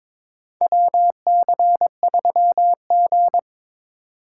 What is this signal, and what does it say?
Morse code, heard as a keyed tone, about 23 wpm, 700 Hz. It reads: WC3G
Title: USA Call Signs - 4 Characters - 2x Character Spacing - 22wpm